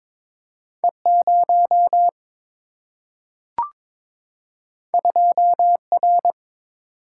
Morse code, heard as a keyed tone, about 22 wpm, 700 Hz. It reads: E0 2R
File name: practice / 22wpm / Two Letter-Number Combinations - 22wpm